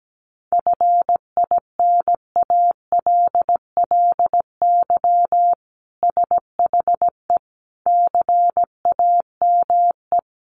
Morse code, heard as a keyed tone, about 17 wpm, 700 Hz. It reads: FINALLY SHE CAME
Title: Sentences from Top 1000 Words 17wpm